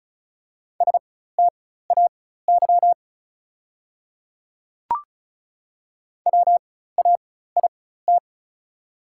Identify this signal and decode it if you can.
Morse code, heard as a keyed tone, about 36 wpm, spaced as for 16 wpm, 690 Hz. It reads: STAY WAIT
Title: 4x Character Spacing - Top 500 Words - 35wpm